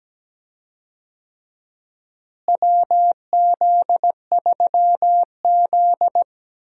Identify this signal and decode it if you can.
Morse code, heard as a keyed tone, about 17 wpm, 700 Hz. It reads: WZ3Z
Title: USA Call Signs - 4 Characters - Repeated 2x - 17wpm